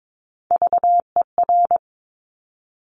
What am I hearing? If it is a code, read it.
Morse code, heard as a keyed tone, about 22 wpm, 700 Hz. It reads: VER
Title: Top Three Letter Combinations in Words 22wpm